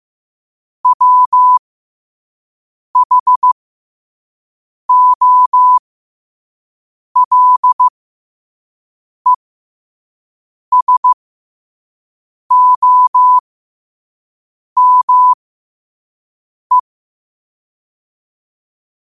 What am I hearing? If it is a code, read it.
Morse code, heard as a keyed tone, about 15 wpm, spaced as for 5 wpm, 1000 Hz. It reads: WHOLESOME
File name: qotdfiles / 5wpm